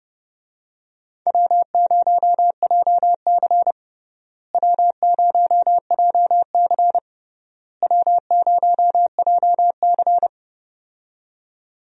Morse code, heard as a keyed tone, about 30 wpm, 700 Hz. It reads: W0JC W0JC W0JC
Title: USA Call Signs - 4 Characters - Repeated 3x - 30wpm